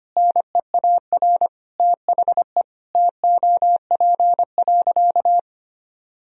Morse code, heard as a keyed tone, about 25 wpm, 700 Hz. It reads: NEAR THE TOP.
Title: Sentences from Top 1000 Words - Review Part 2 - 25wpm